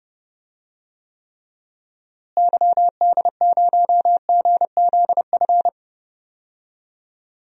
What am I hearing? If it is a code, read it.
Morse code, heard as a keyed tone, about 30 wpm, 700 Hz. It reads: YD0GZF
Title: DXCC-100 Call Signs - 6 Characters - Repeated 3x - 30wpm